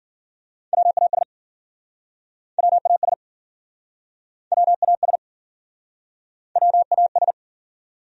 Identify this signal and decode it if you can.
Morse code, heard as a keyed tone, about 52 wpm, 700 Hz. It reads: WAS WAS WAS WAS